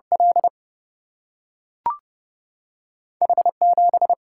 Morse code, heard as a keyed tone, about 30 wpm, 700 Hz. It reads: L H7
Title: Single-Letter Single-Number 30wpm